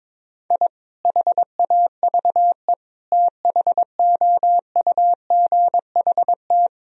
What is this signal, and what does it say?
Morse code, heard as a keyed tone, about 22 wpm, 700 Hz. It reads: I HAVE THOUGHT
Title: Sentences from Top 1000 Words - Review Part 1 - 22wpm